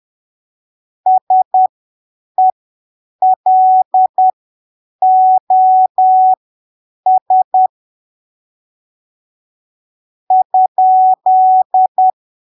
Morse code, heard as a keyed tone, about 10 wpm, spaced as for 6 wpm, 750 Hz. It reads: SELOS?
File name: arquivos / texto / texto 16 10ppm